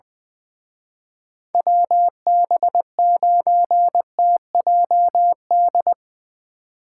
Morse code, heard as a keyed tone, about 23 wpm, 700 Hz. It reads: WB9TJD
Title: Speed Racing - USA Call Signs 20wpm